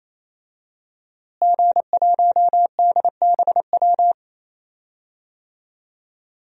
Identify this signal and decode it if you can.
Morse code, heard as a keyed tone, about 28 wpm, 700 Hz. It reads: G1DBW
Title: DXCC-100 Call Signs - 5 Characters - Repeated 3x - 28wpm